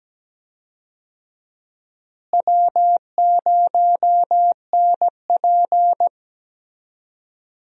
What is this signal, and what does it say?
Morse code, heard as a keyed tone, about 17 wpm, 700 Hz. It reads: W0NP
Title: USA Call Signs - 4 Characters - Repeated 2x - 17wpm